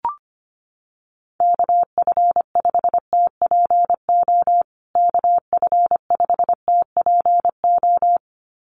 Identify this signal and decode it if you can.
Morse code, heard as a keyed tone, about 25 wpm, 700 Hz. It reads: KF5TPO KF5TPO
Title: USA Call Signs Repeated 2x 25wpm